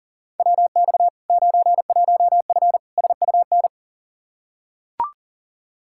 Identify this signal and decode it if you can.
Morse code, heard as a keyed tone, about 37 wpm, 700 Hz. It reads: WX 91F SUN
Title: Practice QSO - Exchange 2 - Info - 40wpm